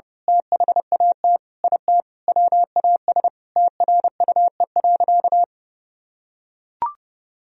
Morse code, heard as a keyed tone, about 32 wpm, 700 Hz. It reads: THAT IT WAS TRUE.
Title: Sentences from Top 400 Words and Abbreviations 30wpm